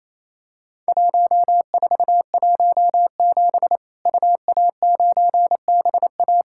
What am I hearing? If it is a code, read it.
Morse code, heard as a keyed tone, about 28 wpm, 700 Hz. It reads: 1417 UA9BA
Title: Worked All Europe DX Contest - QTC Exchange - 28wpm